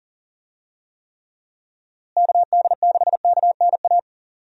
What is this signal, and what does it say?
Morse code, heard as a keyed tone, about 40 wpm, 700 Hz. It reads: KD6KNA
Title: USA Call Signs - 6 Characters - Repeated 2x - 40wpm